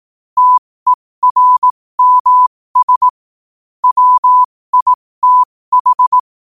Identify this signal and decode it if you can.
Morse code, heard as a keyed tone, about 18 wpm, spaced as for 15 wpm, 1000 Hz. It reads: TERMS WITH
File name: qotdfiles / 15wpm